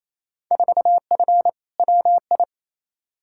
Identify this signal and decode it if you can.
Morse code, heard as a keyed tone, about 28 wpm, 700 Hz. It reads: 4F WI
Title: Field Day Exchanges Encore - 28wpm